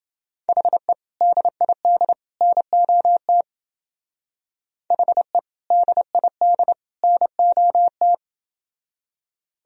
Morse code, heard as a keyed tone, about 28 wpm, 700 Hz. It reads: HE DID NOT HE DID NOT